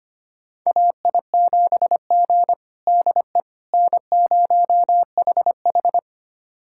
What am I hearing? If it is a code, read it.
Morse code, heard as a keyed tone, about 25 wpm, 700 Hz. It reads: AI7G DE N0HH
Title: USA Callsign DE Callsign - 4 Characters - 25wpm